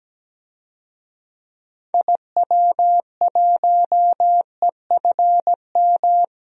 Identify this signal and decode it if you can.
Morse code, heard as a keyed tone, about 17 wpm, 700 Hz. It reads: IW1EFM
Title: DXCC-100 Call Signs - 6 Characters - Repeated 2x - 17wpm